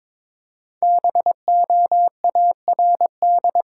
Morse code, heard as a keyed tone, about 22 wpm, 700 Hz. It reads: BOARD
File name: 5 Letter Words - First 3 Letters Given - 22wpm